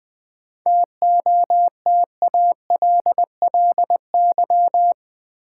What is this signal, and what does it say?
Morse code, heard as a keyed tone, about 20 wpm, 700 Hz. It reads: TOTALLY
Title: Sentences from Top 1600 Words - New Word Drill Alt - 20wpm